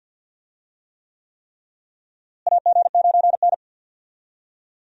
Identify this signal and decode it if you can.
Morse code, heard as a keyed tone, about 51 wpm, 700 Hz. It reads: AG9N
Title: USA Call Signs - 4 Characters - Repeated 3x - 50wpm